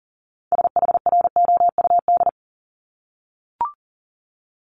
Morse code, heard as a keyed tone, about 39 wpm, 700 Hz. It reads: SHROUD
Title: Multi-letter Phonemes 40wpm